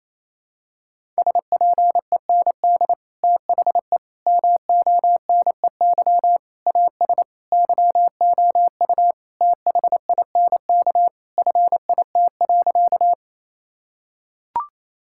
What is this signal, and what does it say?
Morse code, heard as a keyed tone, about 28 wpm, 700 Hz. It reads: SPEND THE MONEY AS YOU THINK FIT.
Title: Sentences from Top 1000 Words - Review Part 5 - 28wpm